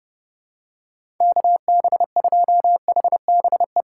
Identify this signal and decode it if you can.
Morse code, heard as a keyed tone, about 31 wpm, 700 Hz. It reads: KB2HBE
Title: USA Call Signs - 4x Character Spacing - 30wpm